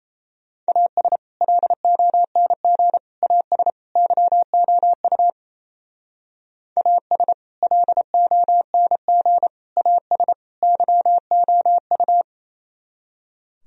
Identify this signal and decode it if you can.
Morse code, heard as a keyed tone, about 36 wpm, 700 Hz. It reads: AS LONG AS YOU AS LONG AS YOU